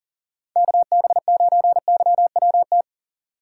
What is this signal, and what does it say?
Morse code, heard as a keyed tone, about 42 wpm, 700 Hz. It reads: KB9YWT
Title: USA Call Signs - 6 Characters - 6x Character Spacing - 40wpm